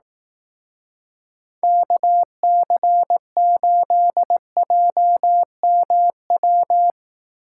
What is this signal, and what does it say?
Morse code, heard as a keyed tone, about 18 wpm, 700 Hz. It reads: KC8JMW
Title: Speed Racing - USA Call Signs 15wpm